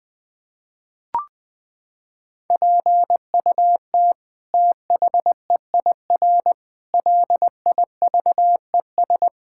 Morse code, heard as a keyed tone, about 20 wpm, 700 Hz. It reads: PUT THEIR LIVES
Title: Sets of 8 Words - 20wpm